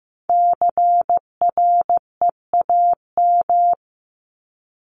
Morse code, heard as a keyed tone, about 15 wpm, 700 Hz. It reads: CREAM
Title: Top 1251-1500 Words 15wpm